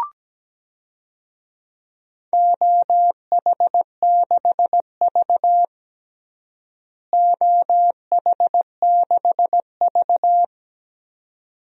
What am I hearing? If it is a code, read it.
Morse code, heard as a keyed tone, about 17 wpm, 700 Hz. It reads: OH6V OH6V
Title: DXCC-100 Call Signs - 4 Characters - Repeated 2x - 17wpm